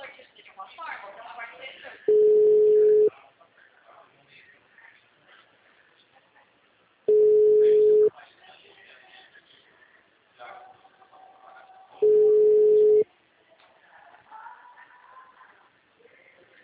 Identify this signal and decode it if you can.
Ringback tone (Europe, 425 Hz), with speech.
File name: out- -2016- - - .81729